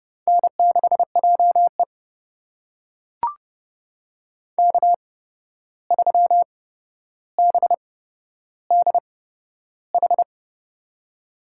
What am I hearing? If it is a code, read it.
Morse code, heard as a keyed tone, about 32 wpm, spaced as for 12 wpm, 700 Hz. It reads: N6JE K3BDH
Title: USA Call Signs - 8x Character Spacing - 30wpm